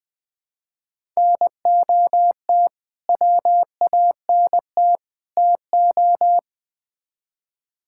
Morse code, heard as a keyed tone, about 20 wpm, 700 Hz. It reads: NOT WANT TO
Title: Build A Phrase - 3-Word Phrases - 20wpm